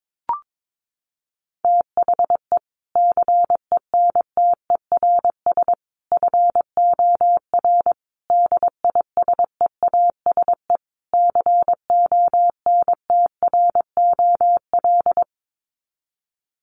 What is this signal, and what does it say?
Morse code, heard as a keyed tone, about 22 wpm, 700 Hz. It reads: THE CENTERS FOR DISEASE CONTROL
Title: Sets of 5 Words Encore2 22wpm